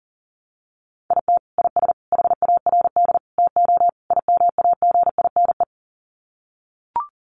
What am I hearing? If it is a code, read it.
Morse code, heard as a keyed tone, about 40 wpm, 700 Hz. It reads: IT IS HARD TO IMAGINE
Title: Sets of 5 Words Encore2 40wpm